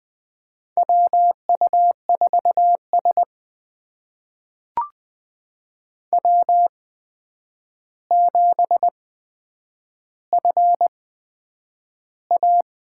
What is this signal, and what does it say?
Morse code, heard as a keyed tone, about 20 wpm, spaced as for 9 wpm, 700 Hz. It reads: WU4S W7FA
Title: USA Call Signs - 8x Character Spacing - 20wpm